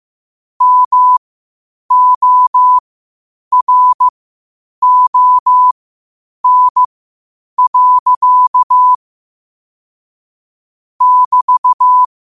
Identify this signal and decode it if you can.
Morse code, heard as a keyed tone, about 15 wpm, spaced as for 8 wpm, 1000 Hz. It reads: MORON. <BT>
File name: qotdfiles / 8wpm